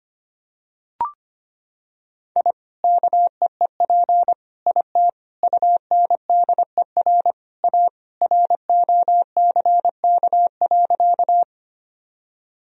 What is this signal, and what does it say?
Morse code, heard as a keyed tone, about 25 wpm, 700 Hz. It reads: I KEEP IT UNDER A ROCK.
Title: Sentences from Top 1000 Words - Review Part 2 - 25wpm